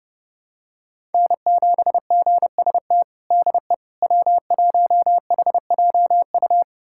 Morse code, heard as a keyed tone, about 30 wpm, 700 Hz. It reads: N7GST DE W1HJU
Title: USA Callsign DE Callsign - 5 Characters - 30wpm